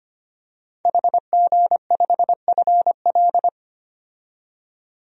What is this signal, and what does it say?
Morse code, heard as a keyed tone, about 25 wpm, 690 Hz. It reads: HG5FL
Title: DXCC-100 Call Signs - 5 Characters - 25wpm